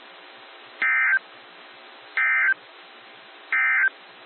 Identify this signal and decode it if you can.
Emergency Alert System (SAME) tones, heard as harsh buzzing bursts. It is heard from 0.7 to 4.0 s.
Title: NOAA Weather Radio - WXJ73